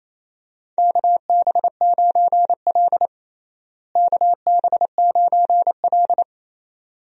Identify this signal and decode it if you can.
Morse code, heard as a keyed tone, about 28 wpm, 700 Hz. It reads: KB9L KB9L
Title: USA Call Signs - 4 Characters - Repeated 2x - 28wpm